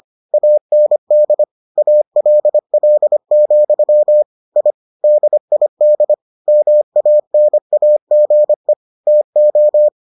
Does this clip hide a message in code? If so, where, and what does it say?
Yes: Morse code from 0.3 to 9.9 s, heard as a keyed tone, about 25 wpm, 600 Hz. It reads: AND ALL, I DID MANAGE TO